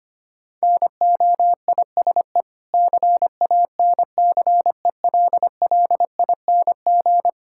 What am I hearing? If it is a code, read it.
Morse code, heard as a keyed tone, about 25 wpm, 700 Hz. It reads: NOISE CANCELLING
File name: practice / 25wpm / Difficult Words with Hints - 25wpm